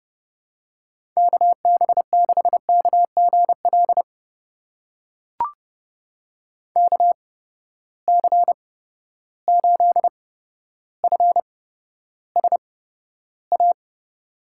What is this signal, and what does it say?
Morse code, heard as a keyed tone, about 30 wpm, spaced as for 13 wpm, 700 Hz. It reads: KB6KGL KC8FSA
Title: USA Call Signs - 6 Characters - 8x Character Spacing - 30wpm